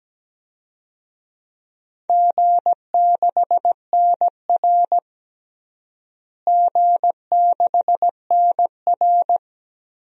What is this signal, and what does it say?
Morse code, heard as a keyed tone, about 17 wpm, 700 Hz. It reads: G6NR G6NR
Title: DXCC-100 Call Signs - 4 Characters - Repeated 3x - 17wpm